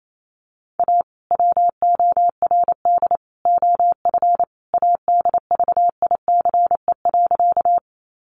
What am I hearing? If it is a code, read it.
Morse code, heard as a keyed tone, about 28 wpm, 700 Hz. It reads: A WORD OF ADVICE.
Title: Sentences from Top 1100 Words 28wpm